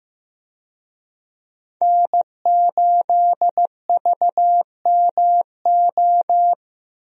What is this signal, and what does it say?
Morse code, heard as a keyed tone, about 15 wpm, 700 Hz. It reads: N8VMO
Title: USA Call Signs - 5 Characters - Repeated 3x - 15wpm